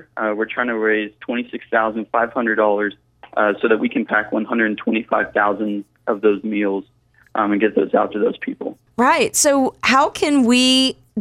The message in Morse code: none